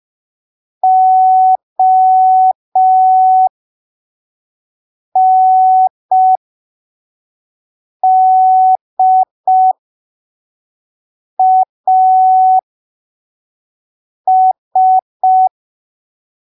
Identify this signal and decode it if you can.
Morse code, heard as a keyed tone, about 5 wpm, spaced as for 3 wpm, 750 Hz. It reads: ONDAS
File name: arquivos / texto / texto 10 05ppm